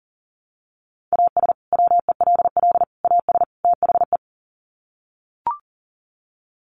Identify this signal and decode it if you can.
Morse code, heard as a keyed tone, about 40 wpm, 700 Hz. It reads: AS WELL AS THE
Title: Sets of 4 Words 40wpm